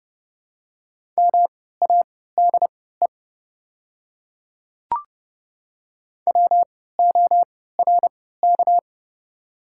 Morse code, heard as a keyed tone, about 30 wpm, spaced as for 17 wpm, 700 Hz. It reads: MADE WORK
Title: 3x Character Spacing - Top 200 Words - 30wpm